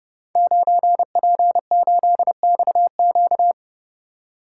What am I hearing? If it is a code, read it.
Morse code, heard as a keyed tone, about 29 wpm, 700 Hz. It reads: 9P8XQ
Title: Code Groups - 5 Letter:Numbers - 30wpm